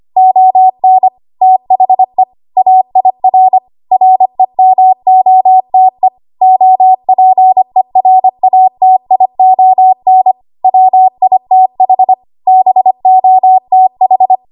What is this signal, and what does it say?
Morse code, heard as a keyed tone, about 25 wpm, 750 Hz. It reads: ON THE AIR REMOTE OPERATION WITH BOTH
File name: Archive / 25 WPM / 25WPM